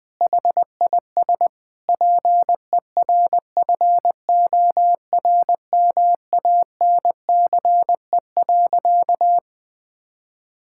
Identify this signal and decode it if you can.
Morse code, heard as a keyed tone, about 20 wpm, 700 Hz. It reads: HIS PERFORMANCE.
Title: Sentences from Top 1600 Words 20wpm